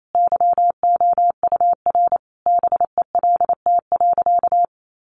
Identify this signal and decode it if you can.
Morse code, heard as a keyed tone, about 28 wpm, 700 Hz. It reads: YOUR BELT.
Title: Sentences from Top 1400 Words 28wpm